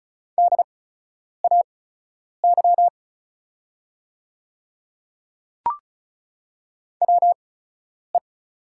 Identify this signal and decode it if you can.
Morse code, heard as a keyed tone, about 36 wpm, spaced as for 10 wpm, 690 Hz. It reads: DAY WE
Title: 8x Character Spacing - Top 200 Words - 35wpm